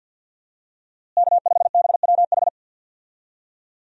Morse code, heard as a keyed tone, about 47 wpm, 700 Hz. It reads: K5BWH
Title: USA Call Signs - 5 Characters - Repeated 2x - 50wpm